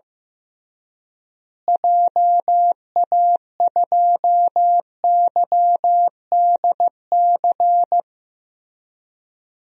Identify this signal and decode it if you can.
Morse code, heard as a keyed tone, about 15 wpm, 700 Hz. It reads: JA2YDC
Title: DXCC-100 Call Signs - 6 Characters - Repeated 3x - 15wpm